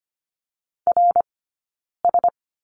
Morse code, heard as a keyed tone, about 25 wpm, spaced as for 9 wpm, 690 Hz. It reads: RS
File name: course / 25:10wpm / 092 Lesson 16d - Letter c - Two Characters 25wpm